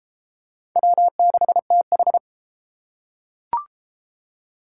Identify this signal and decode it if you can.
Morse code, heard as a keyed tone, about 32 wpm, 700 Hz. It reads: W6TH